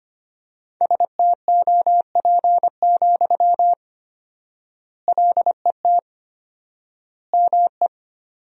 Morse code, heard as a keyed tone, about 25 wpm, 700 Hz. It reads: STOP, LET ME
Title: 4x Word-Spacing - Sentences from Top 300 Words - 25wpm